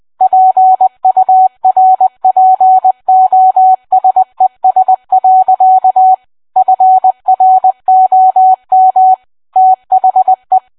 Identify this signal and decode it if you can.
Morse code, heard as a keyed tone, about 20 wpm, 750 Hz. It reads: PURPOSES. FROM THE